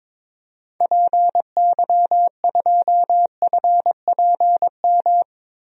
Morse code, heard as a keyed tone, about 22 wpm, 700 Hz. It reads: PY2FPM
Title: DXCC-100 Call Signs - 6 Characters - Repeated 2x - 22wpm